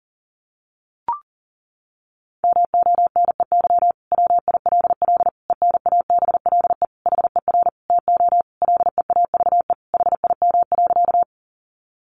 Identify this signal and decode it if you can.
Morse code, heard as a keyed tone, about 40 wpm, 700 Hz. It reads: MONEY WILL ENABLE HER TO LEAVE HIM.
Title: Sentences from Top 1500 Words 40wpm